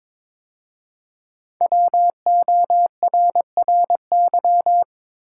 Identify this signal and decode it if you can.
Morse code, heard as a keyed tone, about 23 wpm, 700 Hz. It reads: WORRY
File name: 5 Letter Words - First 3 Letters Given - 22wpm